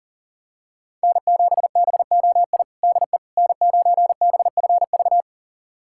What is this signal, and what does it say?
Morse code, heard as a keyed tone, about 40 wpm, 700 Hz. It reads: N7BOI DE N9BFV
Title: USA Callsign DE Callsign - 5 Characters - 40wpm